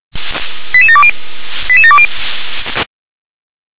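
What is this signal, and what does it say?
Five-tone selective call (ZVEI), heard as quick runs of five tones, tones of about 70 ms. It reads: 88720 88720
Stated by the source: ZVEI five-tone call received by SDR.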